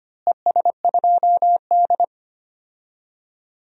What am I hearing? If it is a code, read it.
Morse code, heard as a keyed tone, about 25 wpm, 700 Hz. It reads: ES2D
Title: DXCC-100 Call Signs - 4 Characters - Repeated 3x - 25wpm